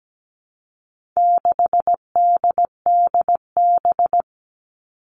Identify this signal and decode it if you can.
Morse code, heard as a keyed tone, about 17 wpm, 700 Hz. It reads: 6DDB
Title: All Cut Numbers - 4-Digit 17wpm